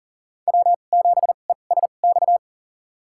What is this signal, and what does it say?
Morse code, heard as a keyed tone, about 40 wpm, spaced as for 29 wpm, 700 Hz. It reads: W7ESX
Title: USA Call Signs - 5 Characters - 2x Character Spacing - 40wpm